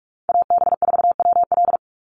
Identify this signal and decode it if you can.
Morse code, heard as a keyed tone, about 44 wpm, 700 Hz. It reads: AB4WL
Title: NA SOTA Chaser Callsigns 45wpm